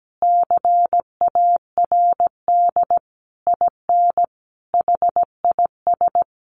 Morse code, heard as a keyed tone, about 17 wpm, 700 Hz. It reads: CARD IN HIS
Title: Sentences from Top 600 Words 17wpm